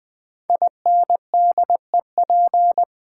Morse code, heard as a keyed tone, about 20 wpm, 700 Hz. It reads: INDEP
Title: Build A Word - 5 Letters At A Time - 12-Letter Words - 20wpm